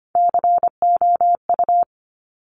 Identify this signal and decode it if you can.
Morse code, heard as a keyed tone, about 25 wpm, 700 Hz. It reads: COU
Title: Top Three Letter Combinations in Words 25wpm